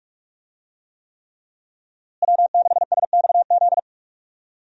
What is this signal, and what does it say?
Morse code, heard as a keyed tone, about 46 wpm, 700 Hz. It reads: W6SXZ
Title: USA Call Signs - 5 Characters - Repeated 3x - 45wpm